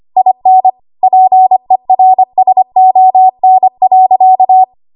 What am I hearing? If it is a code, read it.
Morse code, heard as a keyed tone, about 25 wpm, 750 Hz. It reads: IN PERSON.
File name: Archive / 25 WPM / 25WPM